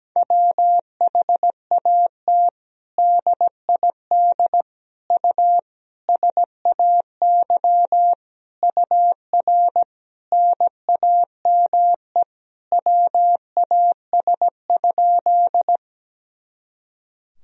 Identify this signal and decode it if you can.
Morse code, heard as a keyed tone, about 17 wpm, 700 Hz. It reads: WHAT DID U SAY UR NAME WAS?
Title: Sentences from Top 200 Words and Abbreviations 17wpm